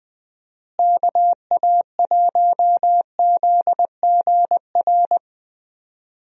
Morse code, heard as a keyed tone, about 21 wpm, 700 Hz. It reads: KA1ZGR
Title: USA Call Signs - 6 Characters - Repeated 3x - 20wpm